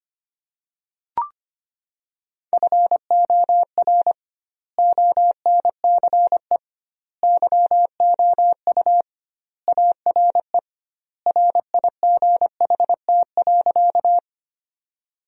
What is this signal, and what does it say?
Morse code, heard as a keyed tone, about 25 wpm, 700 Hz. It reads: FOR ONCE YOU ARE RIGHT.
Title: 2x Word-Spacing - Sentences from Top 300 Words - 25wpm